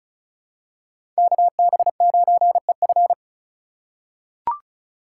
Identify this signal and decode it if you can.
Morse code, heard as a keyed tone, about 36 wpm, 700 Hz. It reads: KB9EF
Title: USA Call Signs - 5 Characters - 3x Character Spacing - 35wpm